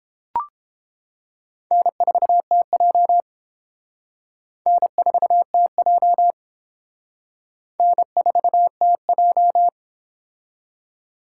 Morse code, heard as a keyed tone, about 31 wpm, 700 Hz. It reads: N4TJ N4TJ N4TJ